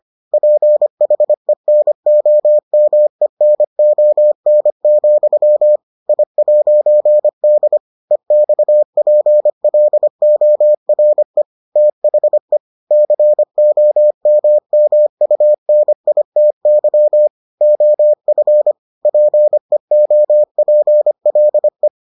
Morse code, heard as a keyed tone, about 25 wpm, 600 Hz. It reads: PHENOMENON, I'D EXPLORE THE COMMUNITY OF PEOPLE